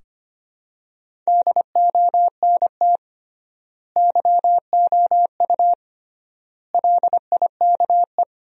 Morse code, heard as a keyed tone, about 25 wpm, 700 Hz. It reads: DONT YOU LIKE
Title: 3x Word-Spacing - Sentences from Top 300 Words - 25wpm